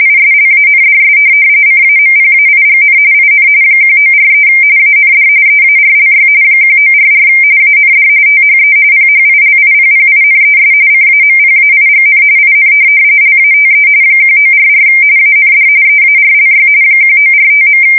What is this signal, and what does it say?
RTTY, about 45.45 baud. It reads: ISLANDS CAMPAIGN IN 1943 AND EARLY 1944 AND THE PHILIPPINES CAMPAIGN IN LATE 1944 AND EARLY 1945.
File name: fsk170 / USS-Massachusetts